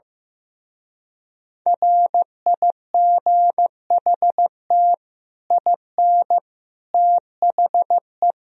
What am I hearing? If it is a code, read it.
Morse code, heard as a keyed tone, about 15 wpm, 700 Hz. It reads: RIGHT IN THE
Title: Speed-Racing - Sets of 5 Words 15wpm